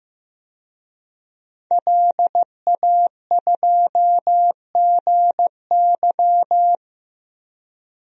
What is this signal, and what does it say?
Morse code, heard as a keyed tone, about 15 wpm, 700 Hz. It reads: LA2GY
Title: DXCC-100 Call Signs - 5 Characters - 15wpm